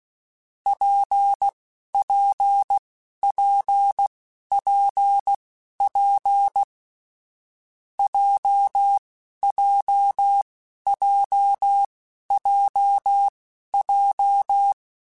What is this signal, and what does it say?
Morse code, heard as a keyed tone, about 16 wpm, spaced as for 11 wpm, 800 Hz. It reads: PPPPP JJJJJ
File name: Difficult Combos / Difficult Combos-mp3s / WPJ WPJ1 @10